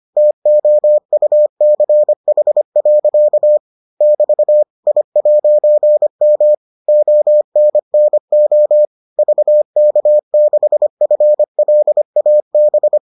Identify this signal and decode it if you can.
Morse code, heard as a keyed tone, about 25 wpm, 600 Hz. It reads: TOUCH. <BT> I'M ONNO VK6FLAB